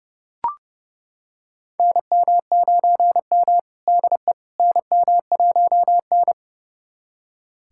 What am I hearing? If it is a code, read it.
Morse code, heard as a keyed tone, about 31 wpm, 700 Hz. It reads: NM9M DE NM1N
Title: USA Callsign DE Callsign - 4 Characters - 30wpm